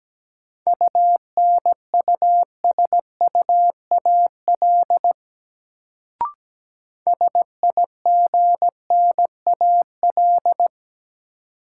Morse code, heard as a keyed tone, about 17 wpm, 700 Hz. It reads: UNUSUAL SIGNAL
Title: Sentences from Top 1400 Words - New Word Drill - 17wpm